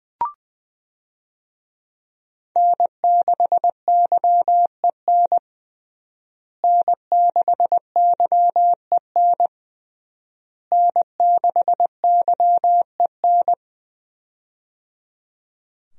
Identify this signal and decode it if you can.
Morse code, heard as a keyed tone, about 20 wpm, 700 Hz. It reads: N6YEN N6YEN N6YEN
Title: USA Call Signs - 5 Characters - Repeated 3x - 20wpm